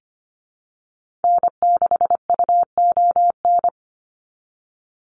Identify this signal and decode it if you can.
Morse code, heard as a keyed tone, about 25 wpm, 700 Hz. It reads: N6UON
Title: USA Call Signs 25wpm